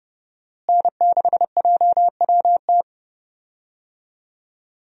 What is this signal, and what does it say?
Morse code, heard as a keyed tone, about 30 wpm, 700 Hz. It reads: N6JWT
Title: USA Call Signs - 5 Characters - Repeated 2x - 30wpm